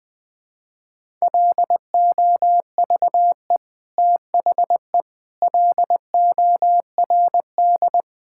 Morse code, heard as a keyed tone, about 20 wpm, 700 Hz. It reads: LOVE THE LORD
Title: Sentences with 9-Word Phrases - Spoken Beginning - 20wpm